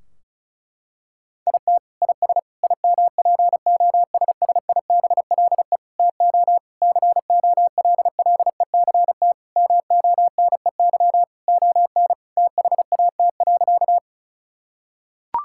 Morse code, heard as a keyed tone, about 36 wpm, 700 Hz. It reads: IT IS IMPOSSIBLE TO COLLECT MONEY ON THAT.
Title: Sentences from Top 1600 Words 35wpm